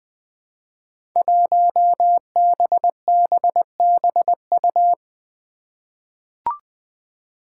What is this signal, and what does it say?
Morse code, heard as a keyed tone, about 20 wpm, 700 Hz. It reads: 1BBBU
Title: Five Challenging Letter-Number Combinations 20wpm